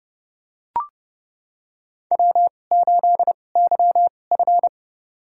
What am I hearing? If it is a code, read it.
Morse code, heard as a keyed tone, about 30 wpm, spaced as for 22 wpm, 700 Hz. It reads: W8YF
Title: USA Call Signs - 2x Character Spacing - 30wpm